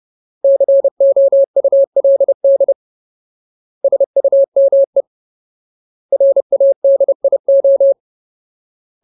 Morse code, heard as a keyed tone, about 30 wpm, 550 Hz. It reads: COULD SUME RADIO
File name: Ultimate Word List / 30WPM Ultimate